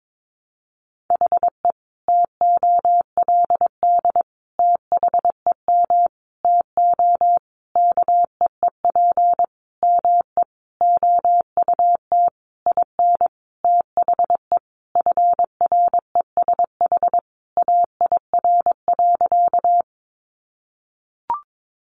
Morse code, heard as a keyed tone, about 22 wpm, 700 Hz. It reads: HE TOLD THEM TO KEEP ME OUT IN THE FRESH AIR.
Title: Sentences from Top 1200 Words 22wpm